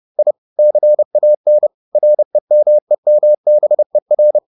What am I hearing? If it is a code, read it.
Morse code, heard as a keyed tone, about 30 wpm, 600 Hz. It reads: I CAN REMEMBER
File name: the old man and the sea / the old man and the sea 1 30wpm